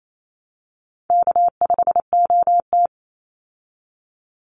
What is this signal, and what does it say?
Morse code, heard as a keyed tone, about 28 wpm, 700 Hz. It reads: K5OT
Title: Call Signs Repeated 2x - 28wpm